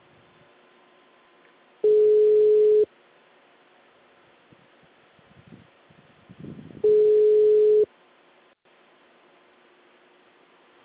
Ringback tone (Europe, 425 Hz).